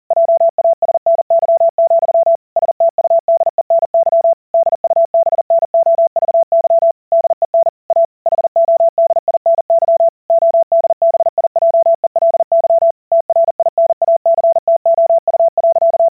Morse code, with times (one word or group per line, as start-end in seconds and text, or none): 0.1-2.4 JASNY,
2.6-4.3 STUDENY
4.5-6.9 DUBNOVY
7.1-7.7 DEN
7.9-8.1 A
8.3-10.1 HODINY
10.3-12.9 ODBIJELY
13.1-16.1 TRINACTOU.